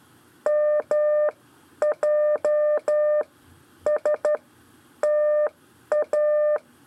Morse code, heard as a keyed tone, about 12 wpm, spaced as for 9 wpm, 600 Hz. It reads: MJSTA
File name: 4 LEZIONE CW 1